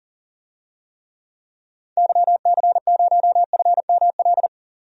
Morse code, heard as a keyed tone, about 37 wpm, 700 Hz. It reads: YC0FML
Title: DXCC-100 Call Signs - 6 Characters - 40wpm